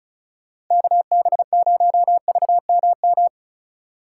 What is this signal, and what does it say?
Morse code, heard as a keyed tone, about 44 wpm, 700 Hz. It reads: KB0VMM